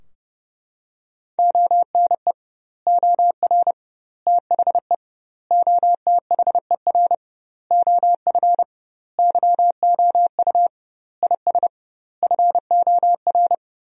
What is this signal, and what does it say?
Morse code, heard as a keyed tone, about 32 wpm, 700 Hz. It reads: ONE OR THE OTHER OF YOU IS FOR
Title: 2x Word-Spacing - Sentences from Top 400 Words - 30wpm